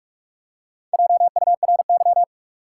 Morse code, heard as a keyed tone, about 43 wpm, 700 Hz. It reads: JURY
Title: Four Letter Words - 45wpm